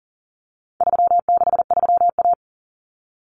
Morse code, heard as a keyed tone, about 40 wpm, 700 Hz. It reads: 363A
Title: Common Cut Numbers - 4-Digit 40wpm